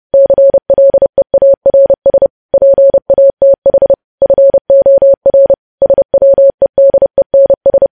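Morse code, heard as a keyed tone, about 30 wpm, 550 Hz. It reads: CLEARS PATH FOR SWEDENS
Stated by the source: News headlines sent in Morse code at 30.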